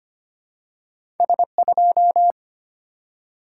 Morse code, heard as a keyed tone, about 25 wpm, 690 Hz. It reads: S2
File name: Two Letter-Number Combinations - 25wpm